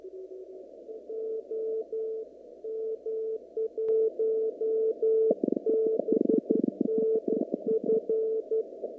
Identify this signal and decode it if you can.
Morse code, heard as a keyed tone, about 12 wpm, 440 Hz. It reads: OM1BNF
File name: CW / kt5j wpx cw-op-k5pi